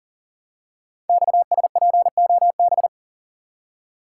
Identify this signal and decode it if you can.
Morse code, heard as a keyed tone, about 41 wpm, 700 Hz. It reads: XSPOB
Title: Code Groups - 5 Letters - 40wpm